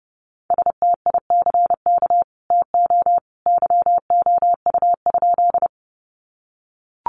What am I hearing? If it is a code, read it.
Morse code, heard as a keyed tone, about 30 wpm, 700 Hz. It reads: STICK TO YOU?
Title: Sentences from Top 1200 Words 30wpm